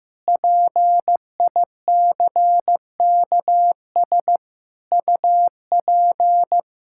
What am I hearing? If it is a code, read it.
Morse code, heard as a keyed tone, about 16 wpm, 700 Hz. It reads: PICKS UP
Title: Pangram Challenge v2 - 15wpm